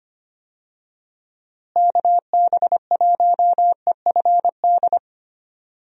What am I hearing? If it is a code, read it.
Morse code, heard as a keyed tone, about 25 wpm, 700 Hz. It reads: KB1EFD
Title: USA Call Signs - 6 Characters - Repeated 2x - 25wpm